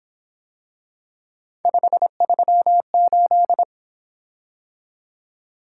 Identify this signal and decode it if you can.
Morse code, heard as a keyed tone, about 26 wpm, 700 Hz. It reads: 538
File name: Speed-Racing - Three Letter-Number Combinations - 22wpm